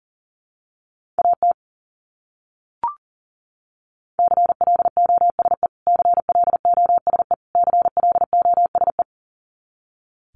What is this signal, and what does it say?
Morse code, heard as a keyed tone, about 40 wpm, 700 Hz. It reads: AT CLOSE CLOSE CLOSE
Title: Top 500 Words Repeated 40wpm